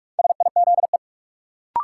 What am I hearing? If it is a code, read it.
Morse code, heard as a keyed tone, about 46 wpm, 700 Hz. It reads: SIZE
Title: Sentences from Top 400 Words - New Word Drill - 45wpm